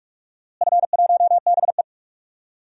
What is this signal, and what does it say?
Morse code, heard as a keyed tone, about 47 wpm, 700 Hz. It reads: F1BE
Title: DXCC-100 Call Signs - 4 Characters - Repeated 2x - 45wpm